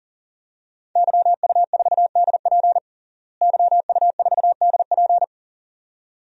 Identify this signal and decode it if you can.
Morse code, heard as a keyed tone, about 40 wpm, 700 Hz. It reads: YU4DP YU4DP
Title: DXCC-100 Call Signs - 5 Characters - Repeated 2x - 40wpm